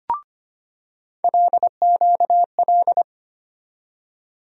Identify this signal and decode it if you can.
Morse code, heard as a keyed tone, about 25 wpm, 700 Hz. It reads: LQL
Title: USA Call Sign Postfixes - 3 Letters 25wpm